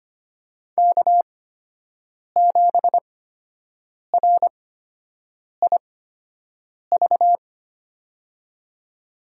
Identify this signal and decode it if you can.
Morse code, heard as a keyed tone, about 25 wpm, spaced as for 7 wpm, 700 Hz. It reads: K7RIV
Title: USA Call Signs - 8x Character Spacing - 25wpm